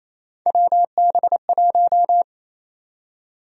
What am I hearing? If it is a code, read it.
Morse code, heard as a keyed tone, about 28 wpm, 700 Hz. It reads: WB1
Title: USA Call Sign Prefixes - Repeated 2x - 28wpm